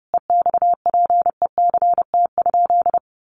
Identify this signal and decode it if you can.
Morse code, heard as a keyed tone, about 30 wpm, 700 Hz. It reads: EXPECT?
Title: Sentences from Top 1100 Words 30wpm